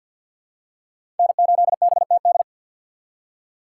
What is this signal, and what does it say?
Morse code, heard as a keyed tone, about 51 wpm, 700 Hz. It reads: N8BTD
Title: USA Call Signs - 5 Characters - 2x Character Spacing - 50wpm